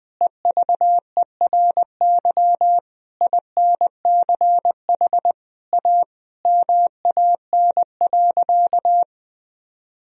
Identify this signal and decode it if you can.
Morse code, heard as a keyed tone, about 20 wpm, 700 Hz. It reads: EVERY INCH A MAN.
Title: Sentences from Top 1000 Words - Review Part 3 - 20wpm